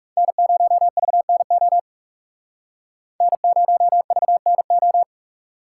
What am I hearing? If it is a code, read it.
Morse code, heard as a keyed tone, about 38 wpm, 700 Hz. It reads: N0VNO N0VNO